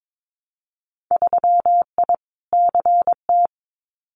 Morse code, heard as a keyed tone, about 22 wpm, 700 Hz. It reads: 3I CT
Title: Winter Field Day Exchanges 22wpm